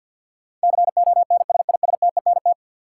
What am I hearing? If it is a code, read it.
Morse code, heard as a keyed tone, about 53 wpm, 700 Hz. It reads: CONSISTENT
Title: Sentences from Top 1500 Words - New Word Drill - 50wpm